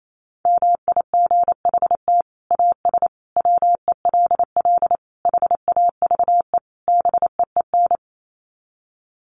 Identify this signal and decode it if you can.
Morse code, heard as a keyed tone, about 28 wpm, 700 Hz. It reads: MIGHT AS WELL HAVE BEEN
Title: Sets of 5 Words Encore2 28wpm